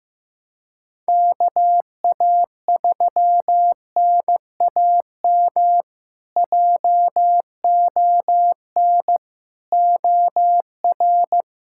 Morse code, heard as a keyed tone, about 15 wpm, 700 Hz. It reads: KA3NAM JON OR
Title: K1USN SST QSO Practice - 15wpm